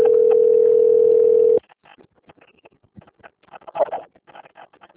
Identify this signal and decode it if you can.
Ringback tone (North America), 440 + 480 Hz.
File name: out- - - - - .28727